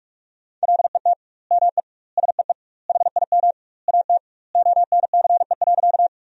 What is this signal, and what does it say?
Morse code, heard as a keyed tone, about 42 wpm, 700 Hz. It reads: LET ME SEE HIM AT ONCE.
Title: 2x Word-Spacing - Sentences from Top 300 Words - 45wpm